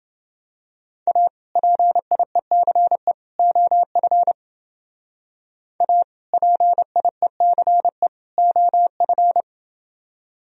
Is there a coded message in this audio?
Yes: Morse code from 1.1 to 9.4 s, heard as a keyed tone, about 29 wpm, 700 Hz. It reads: A PIECE OF A PIECE OF